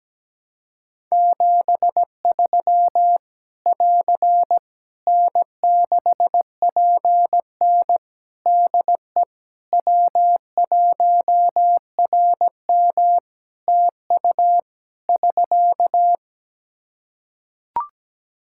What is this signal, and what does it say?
Morse code, heard as a keyed tone, about 17 wpm, 700 Hz. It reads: 73 <AR> N6PN DE W1RM TU <SK>
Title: Practice QSO - Exchange 4 - Ending - 17wpm